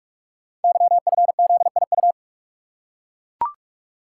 Morse code, heard as a keyed tone, about 44 wpm, 690 Hz. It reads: YFZIU
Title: Code Groups - 5 Letters - 45wpm